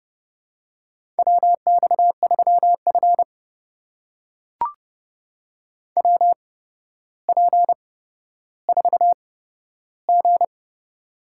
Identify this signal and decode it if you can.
Morse code, heard as a keyed tone, about 31 wpm, spaced as for 13 wpm, 700 Hz. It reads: WX3F WP4G
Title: USA Call Signs - 4 Characters - 8x Character Spacing - 30wpm